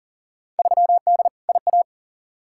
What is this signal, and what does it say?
Morse code, heard as a keyed tone, about 40 wpm, 700 Hz. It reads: 3D IA
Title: Field Day Exchanges Encore - 40wpm